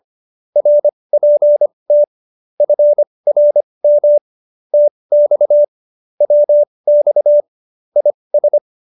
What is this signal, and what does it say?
Morse code, heard as a keyed tone, about 25 wpm, spaced as for 20 wpm, 590 Hz. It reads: RPT FRM TX WX IS